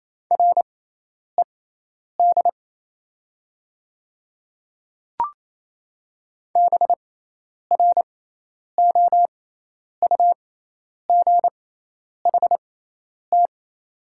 Morse code, heard as a keyed tone, about 28 wpm, spaced as for 10 wpm, 700 Hz. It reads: RED BROUGHT
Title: 6x Character Spacing - Top 500 Words - 28wpm